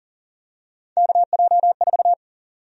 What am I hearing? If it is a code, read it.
Morse code, heard as a keyed tone, about 37 wpm, 700 Hz. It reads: KJ4
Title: USA Call Sign Prefixes - Repeated 2x - 40wpm